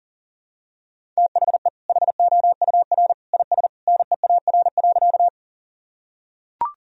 Morse code, heard as a keyed tone, about 40 wpm, 700 Hz. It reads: THE HOUR IS NEAR.
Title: Sentences from Top 1000 Words - Review Part 1 - 40wpm